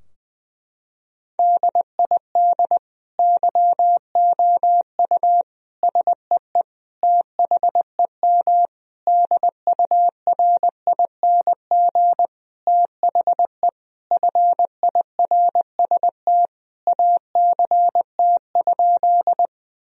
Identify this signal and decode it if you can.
Morse code, heard as a keyed tone, about 20 wpm, 700 Hz. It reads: DID YOU SEE THEM DURING THE FIRST ACT?
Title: Sentences from Top 1000 Words - Review Part 2 - 20wpm